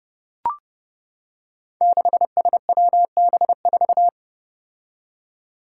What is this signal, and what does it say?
Morse code, heard as a keyed tone, about 29 wpm, 700 Hz. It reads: 6SWB4
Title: Five Challenging Letter-Number Combinations 30wpm